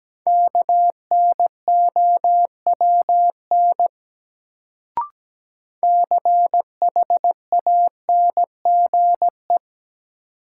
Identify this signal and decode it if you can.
Morse code, heard as a keyed tone, about 17 wpm, 700 Hz. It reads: KNOWN CHANGE
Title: Sentences from Top 600 Words - New Word Drill - 17wpm